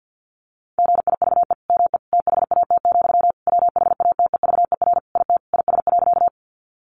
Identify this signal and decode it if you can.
Morse code, heard as a keyed tone, about 50 wpm, 700 Hz. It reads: GIVE ME THAT, WHATEVER IT IS.
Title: Sentences from Top 1100 Words 50wpm